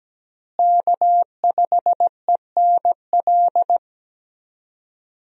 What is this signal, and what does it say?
Morse code, heard as a keyed tone, about 17 wpm, 700 Hz. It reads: K5ENL
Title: USA Call Signs - 5 Characters - Repeated 3x - 17wpm